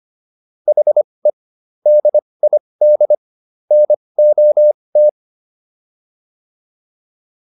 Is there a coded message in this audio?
Yes: Morse code from 0.7 to 5.1 s, heard as a keyed tone, about 25 wpm, spaced as for 20 wpm, 600 Hz. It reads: HE DID NOT